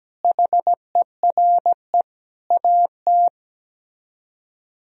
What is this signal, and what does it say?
Morse code, heard as a keyed tone, about 17 wpm, 700 Hz. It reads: HERE AT
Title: Speed-Racing - Sets of 5 Words 17wpm